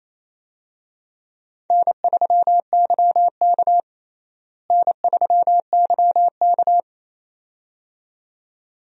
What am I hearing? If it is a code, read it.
Morse code, heard as a keyed tone, about 28 wpm, 700 Hz. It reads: N3YK N3YK
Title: USA Call Signs - 4 Characters - Repeated 2x - 28wpm